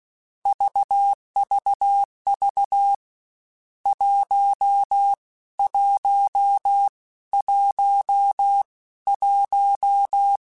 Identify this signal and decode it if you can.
Morse code, heard as a keyed tone, about 16 wpm, spaced as for 13 wpm, 800 Hz. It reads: VVV 1111
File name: Misc / Misc-mp3s / Numb @15